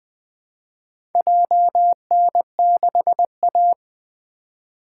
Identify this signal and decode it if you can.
Morse code, heard as a keyed tone, about 20 wpm, 700 Hz. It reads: JN6A
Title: Four Letter-Number Combinations - 20wpm